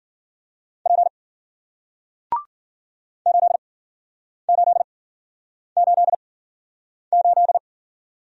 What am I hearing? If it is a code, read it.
Morse code, heard as a keyed tone, about 54 wpm, spaced as for 9 wpm, 700 Hz. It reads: P 8888